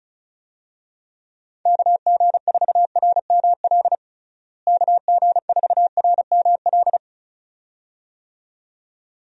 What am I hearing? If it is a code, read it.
Morse code, heard as a keyed tone, about 36 wpm, 700 Hz. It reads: KG4RML KG4RML
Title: USA Call Signs - 6 Characters - Repeated 2x - 35wpm